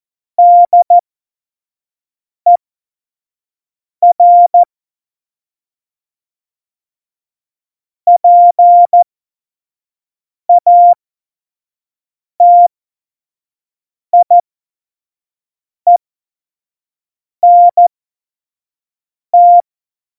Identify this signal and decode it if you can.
Morse code, heard as a keyed tone, about 14 wpm, spaced as for 5 wpm, 700 Hz. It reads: DER PATIENT
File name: Morsekurs / Witze / Witz 27 March 2018, 25 F70